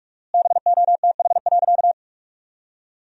Morse code, heard as a keyed tone, about 46 wpm, 700 Hz. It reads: BOTH.
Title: 4x Word-Spacing - Sentences from Top 400 Words - 45wpm